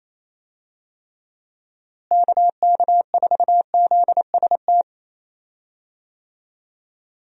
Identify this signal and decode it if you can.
Morse code, heard as a keyed tone, about 28 wpm, 700 Hz. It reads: KK4ZST
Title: USA Call Signs - 6 Characters - Repeated 3x - 28wpm